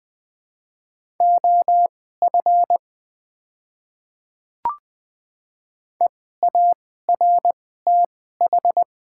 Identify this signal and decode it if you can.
Morse code, heard as a keyed tone, about 20 wpm, spaced as for 14 wpm, 700 Hz. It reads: OF EARTH
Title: 2x Character Spacing - Top 500 Words - 20wpm